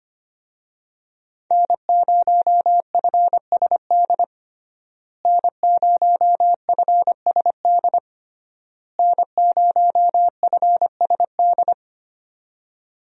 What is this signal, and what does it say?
Morse code, heard as a keyed tone, about 25 wpm, 700 Hz. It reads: N0FSD N0FSD N0FSD
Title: USA Call Signs - 5 Characters - Repeated 3x - 25wpm